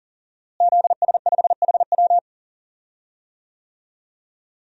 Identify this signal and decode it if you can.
Morse code, heard as a keyed tone, about 37 wpm, 700 Hz. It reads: ZS5HW
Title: DXCC-100 Call Signs - 5 Characters - Repeated 3x - 40wpm